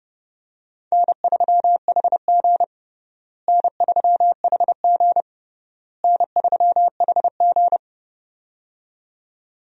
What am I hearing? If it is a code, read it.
Morse code, heard as a keyed tone, about 30 wpm, 700 Hz. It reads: N3HG N3HG N3HG
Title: USA Call Signs - 4 Characters - Repeated 3x - 30wpm